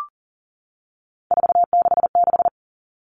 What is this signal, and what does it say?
Morse code, heard as a keyed tone, about 39 wpm, 700 Hz. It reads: 466
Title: All Cut Numbers - 3-Digit 40wpm